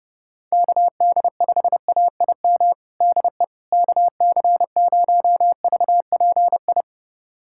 Morse code, heard as a keyed tone, about 30 wpm, 700 Hz. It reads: KD5AIM DE KC0VPI
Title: USA Callsign DE Callsign 30wpm